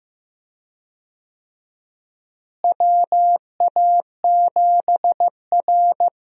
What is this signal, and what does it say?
Morse code, heard as a keyed tone, about 15 wpm, 700 Hz. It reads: WA7R
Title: USA Call Signs - 4 Characters - Repeated 2x - 15wpm